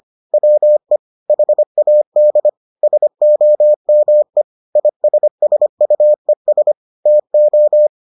Morse code, heard as a keyed tone, about 25 wpm, 600 Hz. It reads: WE HAD SOME ISSUES TO